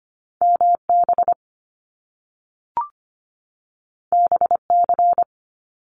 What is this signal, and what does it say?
Morse code, heard as a keyed tone, about 25 wpm, 700 Hz. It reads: MB BC
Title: Canadian Provinces and Territories 25wpm